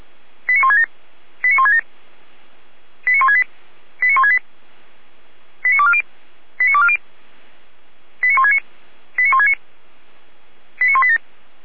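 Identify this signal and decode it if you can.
Five-tone selective call (ZVEI), heard as quick runs of five tones, tones of about 70 ms. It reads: 89167 89167 89168 89168 89240 89240 89169 89169 89187
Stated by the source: ZVEI five-tone selective call received via SDR, fire brigade Feuerwehr Orlamünde.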